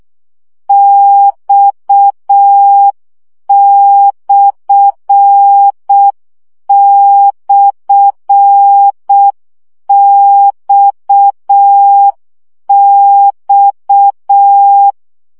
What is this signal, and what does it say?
Morse code, heard as a keyed tone, about 6 wpm, 800 Hz. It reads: X//XX
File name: CW gyakorlas x per 5 baud